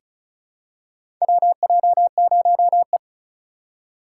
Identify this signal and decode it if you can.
Morse code, heard as a keyed tone, about 34 wpm, 700 Hz. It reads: WJ0E
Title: USA Call Signs - 4 Characters - 3x Character Spacing - 35wpm